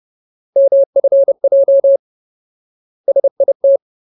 Morse code, heard as a keyed tone, about 30 wpm, 550 Hz. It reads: MFJ SIT
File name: Ultimate Word List / 30WPM Ultimate